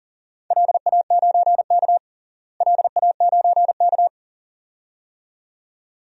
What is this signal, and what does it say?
Morse code, heard as a keyed tone, about 40 wpm, 700 Hz. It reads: LA9K LA9K
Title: DXCC-100 Call Signs - 4 Characters - Repeated 2x - 40wpm